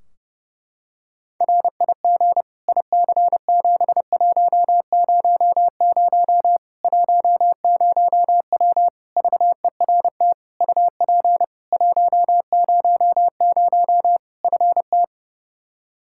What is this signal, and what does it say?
Morse code, heard as a keyed tone, about 32 wpm, 700 Hz. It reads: RIG IC7100 10W VERT UP 100 FT
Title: Practice QSO - Exchange 2 - Info - 30wpm